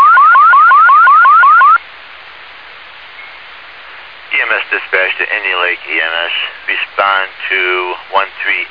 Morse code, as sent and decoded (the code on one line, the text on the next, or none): none